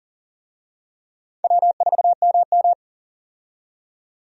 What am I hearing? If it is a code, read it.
Morse code, heard as a keyed tone, about 40 wpm, 700 Hz. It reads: W4MM
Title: USA Call Signs - 4 Characters - 40wpm